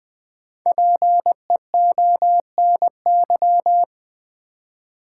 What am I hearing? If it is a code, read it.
Morse code, heard as a keyed tone, about 20 wpm, 690 Hz. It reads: PEONY
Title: Difficult Words with Hints - 20wpm